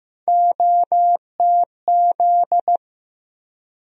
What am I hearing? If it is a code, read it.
Morse code, heard as a keyed tone, about 15 wpm, 700 Hz. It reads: OTZ
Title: USA Call Sign Postfixes - 3 Letters 15wpm